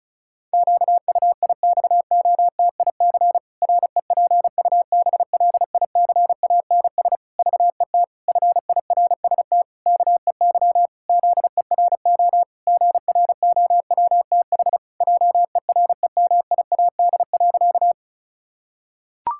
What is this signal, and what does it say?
Morse code, heard as a keyed tone, about 34 wpm, 700 Hz. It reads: QUIXOTIC REPUBLICANS VET FIRST KEY ZERO GROWTH JEREMIAD.
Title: Pangram Challenge v2 - 35wpm